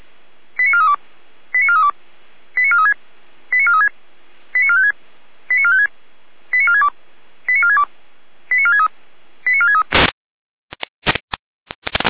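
Five-tone selective call (ZVEI), heard as quick runs of five tones, tones of about 80 ms. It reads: 89432 89432 89436 89436 89456 89456 89462 89462 89463 89463
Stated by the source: ZVEI five-tone selective call received by SDR, from the Feuerwehr Orlamünde fire brigade audio server.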